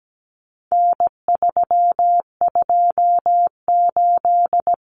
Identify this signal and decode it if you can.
Morse code, heard as a keyed tone, about 17 wpm, 700 Hz. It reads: N328
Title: Common Cut Numbers - 4-Digit 17wpm